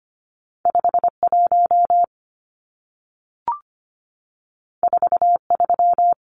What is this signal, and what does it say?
Morse code, heard as a keyed tone, about 25 wpm, 700 Hz. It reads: 51 43
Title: Numbers - 2-Digit 25wpm